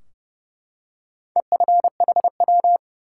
Morse code, heard as a keyed tone, about 30 wpm, 700 Hz. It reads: EFHW
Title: Top 180 QSO Elements 30wpm